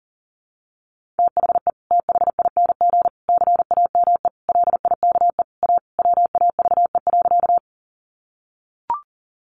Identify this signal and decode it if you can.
Morse code, heard as a keyed tone, about 40 wpm, 700 Hz. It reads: THE THING CAME LIKE A WAVE.
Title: Sentences from Top 1400 Words 40wpm